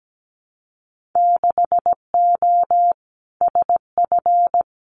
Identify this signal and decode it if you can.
Morse code, heard as a keyed tone, about 17 wpm, 700 Hz. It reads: 6O SF
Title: Winter Field Day Exchanges Encore 17wpm